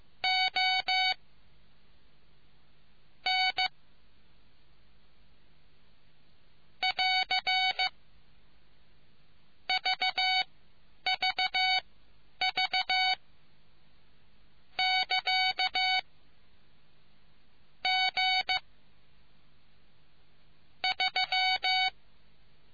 Morse code, heard as a keyed tone, about 15 wpm, spaced as for 9 wpm, 750 Hz. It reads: O N <AR> VVV <KA> G 3